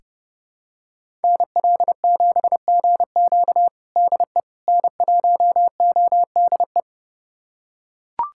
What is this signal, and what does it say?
Morse code, heard as a keyed tone, about 30 wpm, 700 Hz. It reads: NL7GQ DE N1ODE
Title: USA Callsign DE Callsign - 5 Characters - 30wpm